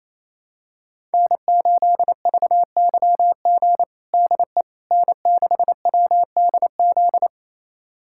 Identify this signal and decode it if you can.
Morse code, heard as a keyed tone, about 28 wpm, 700 Hz. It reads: N8VYG DE N6WDZ
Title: USA Callsign DE Callsign - 5 Characters - 28wpm